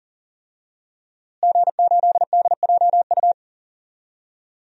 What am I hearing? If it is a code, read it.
Morse code, heard as a keyed tone, about 38 wpm, 700 Hz. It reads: G8DJU
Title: DXCC-100 Call Signs - 5 Characters - Repeated 3x - 40wpm